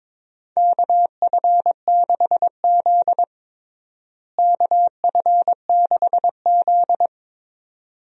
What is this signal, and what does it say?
Morse code, heard as a keyed tone, about 22 wpm, 700 Hz. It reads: KF6Z KF6Z
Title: USA Call Signs - 4 Characters - Repeated 2x - 22wpm